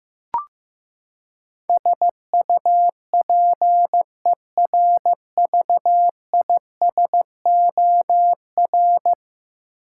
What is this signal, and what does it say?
Morse code, heard as a keyed tone, about 15 wpm, 700 Hz. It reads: SUPERVISOR
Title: Practice QSO - Exchange 3 - More Info - 15wpm